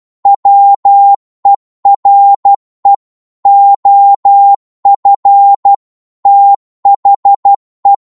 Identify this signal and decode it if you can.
Morse code, heard as a keyed tone, about 12 wpm, 800 Hz. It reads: WERE OF THE